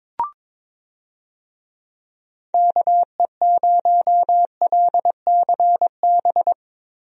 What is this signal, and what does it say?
Morse code, heard as a keyed tone, about 22 wpm, 700 Hz. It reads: KE0LCB
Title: USA Call Signs - 6 Characters - Repeated 3x - 22wpm